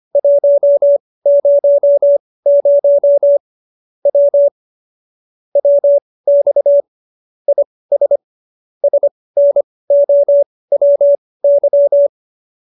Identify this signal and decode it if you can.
Morse code, heard as a keyed tone, about 25 wpm, spaced as for 18 wpm, 570 Hz. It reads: 100 W WX IS SNOWY